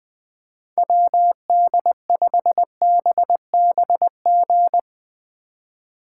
Morse code, heard as a keyed tone, about 20 wpm, 700 Hz. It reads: WD5BBG
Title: Speed Racing - USA 2x3 Call Signs 17wpm